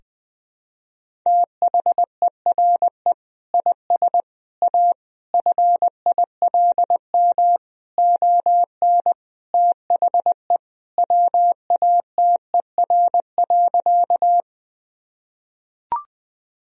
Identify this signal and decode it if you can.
Morse code, heard as a keyed tone, about 20 wpm, 700 Hz. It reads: THERE IS A FILM ON THE WATER.
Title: Sentences from Top 1000 Words - Review Part 3 - 20wpm